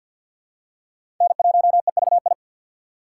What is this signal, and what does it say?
Morse code, heard as a keyed tone, about 50 wpm, 700 Hz. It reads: N1EVI
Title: USA Call Signs - 5 Characters - 6x Character Spacing - 50wpm